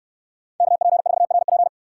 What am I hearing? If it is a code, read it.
Morse code, heard as a keyed tone, about 68 wpm, 700 Hz. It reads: KP4RP